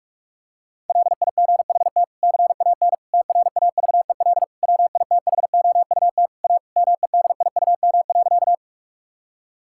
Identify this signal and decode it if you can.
Morse code, heard as a keyed tone, about 42 wpm, 700 Hz. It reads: LIGHT CAN TRAVEL WITHOUT A MEDIUM.
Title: Sentences from Top 1000 Words - Review Part 6 - 45wpm